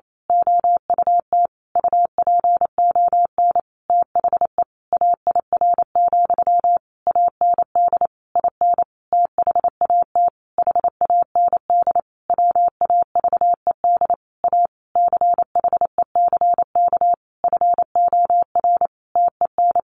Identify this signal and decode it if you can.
Morse code, heard as a keyed tone, about 28 wpm, 700 Hz. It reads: OUT UPON THE AIR, AND IN THAT HAND WAVED A CHECK FOR TEN
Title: A Sound Of Thunder 28wpm